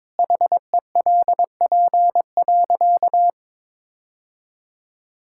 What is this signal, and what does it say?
Morse code, heard as a keyed tone, about 21 wpm, 700 Hz. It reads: HELP.
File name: practice / 22wpm / Sentences with 8-Word Phrases - 22wpm